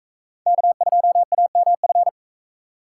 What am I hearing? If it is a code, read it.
Morse code, heard as a keyed tone, about 44 wpm, 700 Hz. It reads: K2AMF